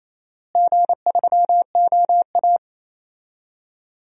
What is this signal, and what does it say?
Morse code, heard as a keyed tone, about 28 wpm, 700 Hz. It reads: G3OA
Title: DXCC-100 Call Signs - 4 Characters - Repeated 3x - 28wpm